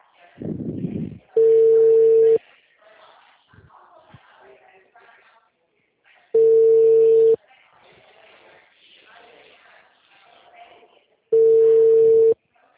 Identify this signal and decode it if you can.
Ringback tone (China, 450 Hz), with speech.